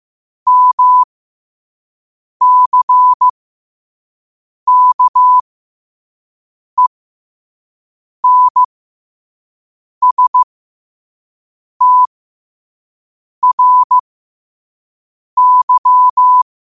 Morse code, heard as a keyed tone, about 15 wpm, spaced as for 5 wpm, 1000 Hz. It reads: MCKENSTRY
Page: Quotes for Wed, 13 Aug 2025 in Morse Code at 5 words per minute.